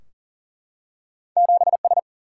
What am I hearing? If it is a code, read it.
Morse code, heard as a keyed tone, about 40 wpm, 700 Hz. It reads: 7S
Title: Single-Letter Single-Number 40wpm